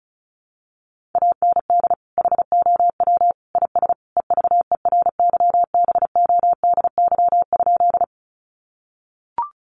Morse code, heard as a keyed tone, about 34 wpm, 700 Hz. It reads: AND HOW IS EVERYBODY?
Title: Sentences from Top 1400 Words 35wpm